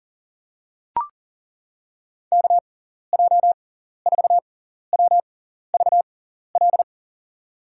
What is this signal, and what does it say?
Morse code, heard as a keyed tone, about 38 wpm, spaced as for 14 wpm, 700 Hz. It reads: KJ4WVL
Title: USA Call Signs - 6 Characters - 6x Character Spacing - 40wpm